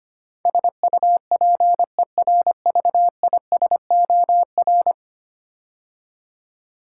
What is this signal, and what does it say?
Morse code, heard as a keyed tone, about 25 wpm, 700 Hz. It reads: SUPERVISOR
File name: 10 Letter Words - First 3 Letters Given - 25wpm